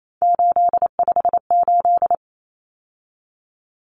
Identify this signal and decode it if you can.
Morse code, heard as a keyed tone, about 28 wpm, 700 Hz. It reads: 858
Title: Numbers - 3-Digit 28wpm